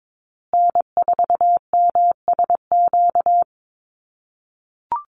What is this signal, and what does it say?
Morse code, heard as a keyed tone, about 22 wpm, 700 Hz. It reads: N4MSQ
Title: USA Call Signs Repeated 2x 22wpm